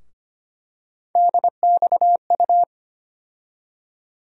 Morse code, heard as a keyed tone, about 25 wpm, 700 Hz. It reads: DXU
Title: USA Call Sign Postfixes - 3 Letters 25wpm